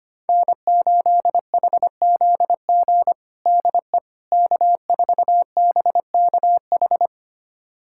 Morse code, heard as a keyed tone, about 25 wpm, 700 Hz. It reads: N8HZG DE K4BKH
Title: USA Callsign DE Callsign 25wpm